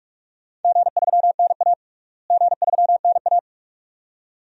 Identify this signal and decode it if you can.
Morse code, heard as a keyed tone, about 45 wpm, 700 Hz. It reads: G3NA G3NA
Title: DXCC-100 Call Signs - 4 Characters - Repeated 2x - 45wpm